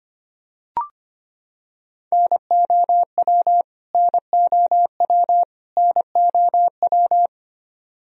Morse code, heard as a keyed tone, about 25 wpm, 700 Hz. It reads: NOW NOW NOW
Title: Top 100 Words Repeated 25wpm